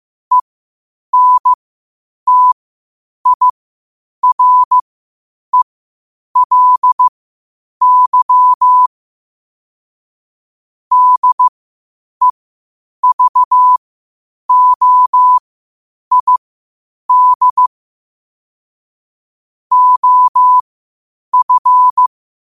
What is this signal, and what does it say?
Morse code, heard as a keyed tone, about 15 wpm, spaced as for 8 wpm, 1000 Hz. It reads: ENTIRELY DEVOID OF
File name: qotdfiles / 8wpm